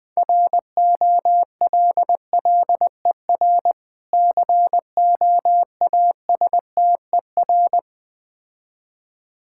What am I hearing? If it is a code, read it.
Morse code, heard as a keyed tone, about 21 wpm, 700 Hz. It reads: ROLLER COASTER
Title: Easy Words - Speed-Racing with Hints - 20wpm